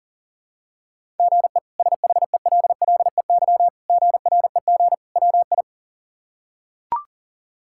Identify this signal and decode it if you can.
Morse code, heard as a keyed tone, about 40 wpm, 700 Hz. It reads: GE SHELLEY GREG WI
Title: K1USN SST QSO Practice - 40wpm